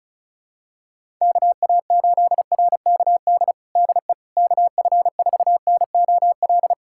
Morse code, heard as a keyed tone, about 36 wpm, 700 Hz. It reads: KA8RKD DE KF4NOL
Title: USA Callsign DE Callsign - 6 Characters - 35wpm